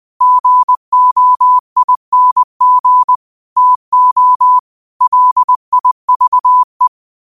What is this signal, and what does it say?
Morse code, heard as a keyed tone, about 20 wpm, 1000 Hz. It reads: GOING TO LIVE
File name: qotdfiles / 20wpm